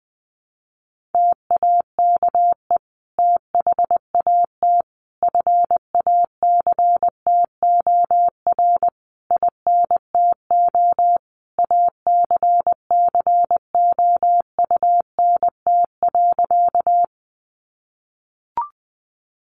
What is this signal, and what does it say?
Morse code, heard as a keyed tone, about 20 wpm, 700 Hz. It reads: TAKE THAT FACTOR INTO ACCOUNT.
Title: Sentences from Top 1300 Words 20wpm